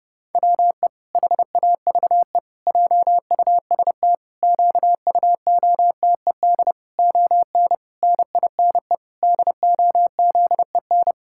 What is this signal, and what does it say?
Morse code, heard as a keyed tone, about 30 wpm, 700 Hz. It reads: WE HAVE JUST QUOTED ON NINE DOZEN
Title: Pangram Challenge v2 - 30wpm